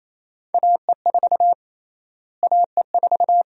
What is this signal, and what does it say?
Morse code, heard as a keyed tone, about 28 wpm, 700 Hz. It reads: AE4 AE4
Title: USA Call Sign Prefixes - Repeated 2x - 28wpm